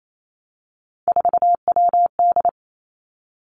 Morse code, heard as a keyed tone, about 28 wpm, 700 Hz. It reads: 4WD
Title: Three Challenging Letter-Number Combinations 28wpm